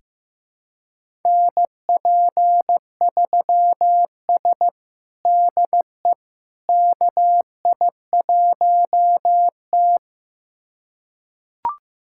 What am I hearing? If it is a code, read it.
Morse code, heard as a keyed tone, about 16 wpm, 700 Hz. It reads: NP3S DE KI1T
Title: USA Callsign DE Callsign - 4 Characters - 15wpm